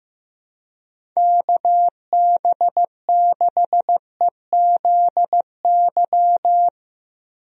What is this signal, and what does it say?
Morse code, heard as a keyed tone, about 16 wpm, 700 Hz. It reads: KB6EZY
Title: USA Call Signs - 6 Characters - Repeated 3x - 15wpm